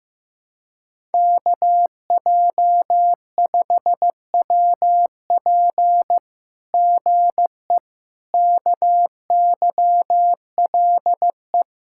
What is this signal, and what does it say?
Morse code, heard as a keyed tone, about 15 wpm, 700 Hz. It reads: KJ5WP GE KYLE
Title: SOTA QSO Practice 15wpm